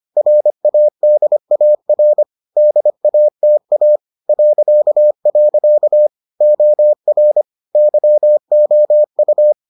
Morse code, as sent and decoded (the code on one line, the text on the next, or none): .-. .- -.. .- .-. / -.. .- - .- / .-.-.- .-.-.- / --- .-. / -.-- --- ..-
RADAR DATA .. OR YOU